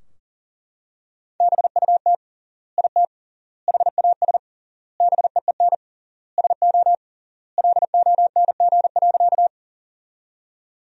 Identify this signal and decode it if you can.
Morse code, heard as a keyed tone, about 40 wpm, 700 Hz. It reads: BUT IT HAS BEEN SO LONG.
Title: 3x Word-Spacing - Sentences from Top 100 Words - 40wpm